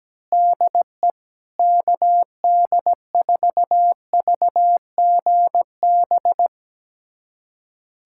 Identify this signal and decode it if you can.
Morse code, heard as a keyed tone, about 17 wpm, 700 Hz. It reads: DE KD4VGB
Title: USA Callsign DE Callsign 17wpm